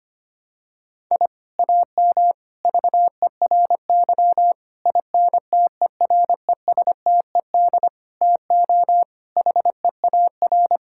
Morse code, heard as a keyed tone, about 25 wpm, 700 Hz. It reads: I AM VERY INTERESTED TO HEAR
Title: Sentences from Top 1000 Words - Review Part 6 - 25wpm